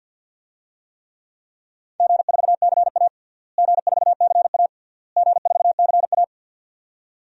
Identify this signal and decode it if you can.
Morse code, heard as a keyed tone, about 48 wpm, 700 Hz. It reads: G4CA G4CA G4CA
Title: DXCC-100 Call Signs - 4 Characters - Repeated 3x - 50wpm